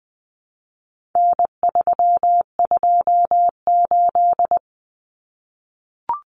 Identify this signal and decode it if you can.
Morse code, heard as a keyed tone, about 20 wpm, 700 Hz. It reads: N328
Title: Common Cut Numbers - 4-Digit 20wpm